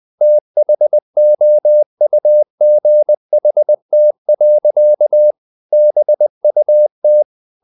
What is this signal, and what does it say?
Morse code, heard as a keyed tone, about 20 wpm, 600 Hz. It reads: THOUGHT. BUT
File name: the old man and the sea part 2 20wpm